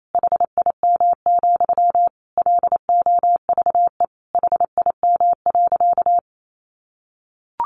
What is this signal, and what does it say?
Morse code, heard as a keyed tone, about 28 wpm, 700 Hz. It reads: HIM, LOVE HIM.
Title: Sentences from Top 1100 Words 28wpm